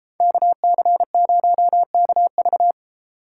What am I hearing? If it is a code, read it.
Morse code, heard as a keyed tone, about 33 wpm, 700 Hz. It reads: KC0KV